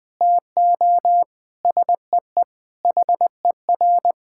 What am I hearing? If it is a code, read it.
Morse code, heard as a keyed tone, about 19 wpm, 700 Hz. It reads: TO SEE HER
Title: Sentences from Top 1000 Words - Review Part 5 - 20wpm